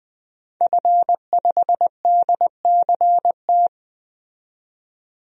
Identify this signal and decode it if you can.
Morse code, heard as a keyed tone, about 20 wpm, 700 Hz. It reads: F5DCT
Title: DXCC-100 Call Signs - 5 Characters - 20wpm